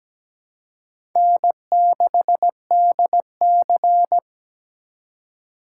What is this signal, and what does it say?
Morse code, heard as a keyed tone, about 20 wpm, 700 Hz. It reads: N6DC
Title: Speed Racing - USA 1x2 Call Signs 17wpm